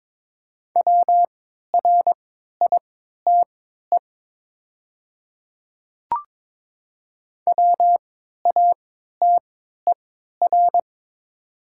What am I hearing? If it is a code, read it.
Morse code, heard as a keyed tone, about 23 wpm, spaced as for 13 wpm, 700 Hz. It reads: WRITE WATER
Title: 3x Character Spacing - Top 200 Words - 22wpm